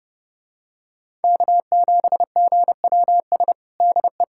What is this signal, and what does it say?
Morse code, heard as a keyed tone, about 32 wpm, 700 Hz. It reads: K7GWS DE
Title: USA Callsign DE Callsign - 5 Characters - 30wpm